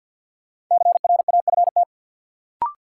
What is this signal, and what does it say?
Morse code, heard as a keyed tone, about 49 wpm, 700 Hz. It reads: CRAFT
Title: Sentences from Top 1000 Words - Review Part 6 - New Word Drill - 50wpm